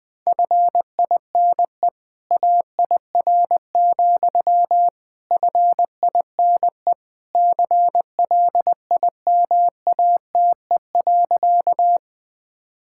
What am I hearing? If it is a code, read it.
Morse code, heard as a keyed tone, about 20 wpm, 700 Hz. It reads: FINE AIR, FINE CLIMATE.
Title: Sentences from Top 1600 Words 20wpm